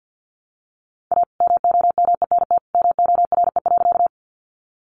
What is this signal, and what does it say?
Morse code, heard as a keyed tone, about 50 wpm, 700 Hz. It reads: A MOMENT MORE.
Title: Sentences from Top 1000 Words 50wpm